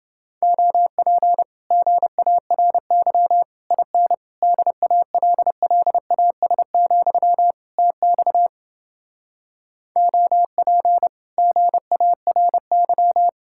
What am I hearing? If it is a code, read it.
Morse code, heard as a keyed tone, about 30 wpm, 700 Hz. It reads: OP GARY IN DALLAS, TX OP GARY